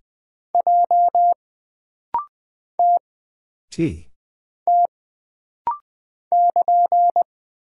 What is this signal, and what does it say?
Morse code, heard as a keyed tone, about 20 wpm, 700 Hz. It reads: J T T <KN>
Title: Single Character (Letter:Number:Punctuation:Prosign) - Rapid-Fire - 20wpm